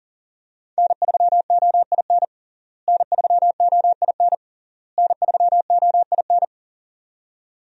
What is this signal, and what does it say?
Morse code, heard as a keyed tone, about 40 wpm, 700 Hz. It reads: N3OIN N3OIN N3OIN
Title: USA Call Signs - 5 Characters - Repeated 3x - 40wpm